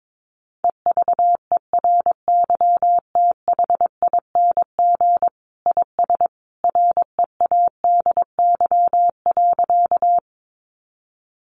Morse code, heard as a keyed tone, about 22 wpm, 700 Hz. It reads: EVERYTHING IS READY.
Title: Sentences from Top 700 Words 22wpm